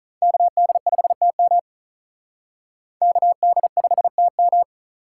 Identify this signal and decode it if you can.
Morse code, heard as a keyed tone, about 39 wpm, 700 Hz. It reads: KD5TM KD5TM
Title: Speed Racing - USA Call Signs 35wpm